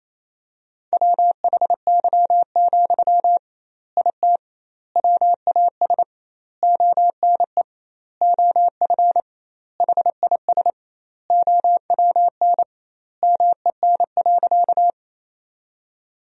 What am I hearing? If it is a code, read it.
Morse code, heard as a keyed tone, about 28 wpm, 700 Hz. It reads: WHY, IT WAS ONE OF HIS OWN MEN.
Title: 2x Word-Spacing - Sentences from Top 300 Words - 28wpm